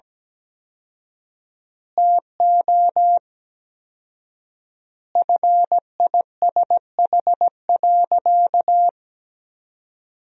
Morse code, heard as a keyed tone, about 17 wpm, 700 Hz. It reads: TO FISH.
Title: 4x Word-Spacing - Sentences from Top 300 Words - 17wpm